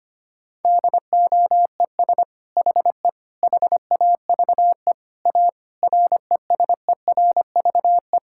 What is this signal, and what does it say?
Morse code, heard as a keyed tone, about 25 wpm, 700 Hz. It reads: DOES HE HAVE A RESERVE
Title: Sentences from Top 1000 Words - Review Part 6 - 25wpm